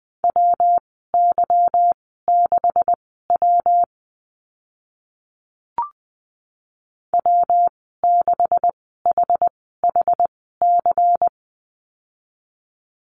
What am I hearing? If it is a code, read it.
Morse code, heard as a keyed tone, about 20 wpm, spaced as for 14 wpm, 700 Hz. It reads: WY6W W6HHC
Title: 161 Lesson 26g - Letter g - Call Signs 20wpm